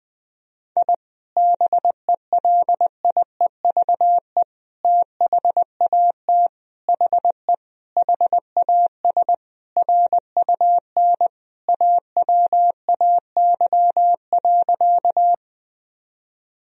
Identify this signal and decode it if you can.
Morse code, heard as a keyed tone, about 20 wpm, 700 Hz. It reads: I BELIEVE THAT HE HAS RUN AWAY.
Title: Sentences from Top 1000 Words - Review Part 1 - 20wpm